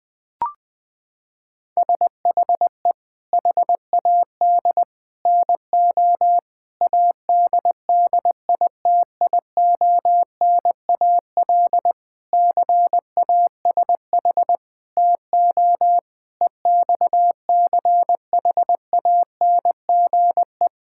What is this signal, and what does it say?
Morse code, heard as a keyed tone, about 20 wpm, 700 Hz. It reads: SHE HAD NO ADDITIONAL CASH TO EXCHANGE
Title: Sentences from Top 1000 Words - Review Part 6 - 20wpm